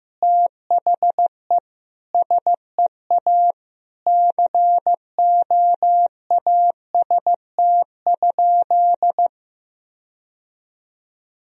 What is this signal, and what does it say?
Morse code, heard as a keyed tone, about 15 wpm, 700 Hz. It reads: THE SEA COAST?
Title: Sentences from Top 1000 Words - Review Part 4 - 15wpm